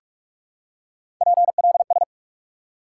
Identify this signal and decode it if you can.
Morse code, heard as a keyed tone, about 46 wpm, 700 Hz. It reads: PLS
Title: Top 90 QSO Elements 45wpm